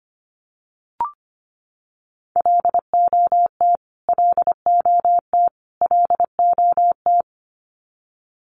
Morse code, heard as a keyed tone, about 25 wpm, 700 Hz. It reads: LOT LOT LOT
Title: Top 500 Words Repeated 25wpm